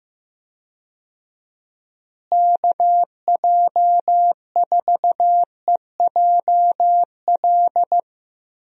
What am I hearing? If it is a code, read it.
Morse code, heard as a keyed tone, about 15 wpm, 700 Hz. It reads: KJ4EJL
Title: USA Call Signs - 6 Characters - Repeated 2x - 15wpm